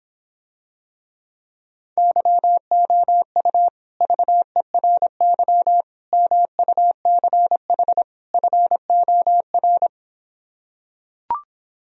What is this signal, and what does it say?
Morse code, heard as a keyed tone, about 26 wpm, 700 Hz. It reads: YOU VERY MUCH FOR